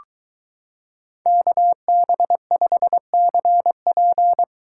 Morse code, heard as a keyed tone, about 23 wpm, 700 Hz. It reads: KB5CP